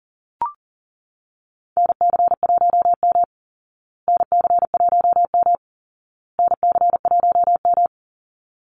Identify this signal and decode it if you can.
Morse code, heard as a keyed tone, about 40 wpm, 700 Hz. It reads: NC1M NC1M NC1M
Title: Call Signs Repeated 2x - 40wpm